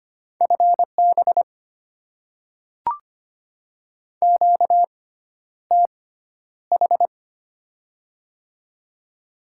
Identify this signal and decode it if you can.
Morse code, heard as a keyed tone, about 25 wpm, spaced as for 11 wpm, 700 Hz. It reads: FB QTH
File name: Top 20 QSO Elements - 6x Character Spacing - 25wpm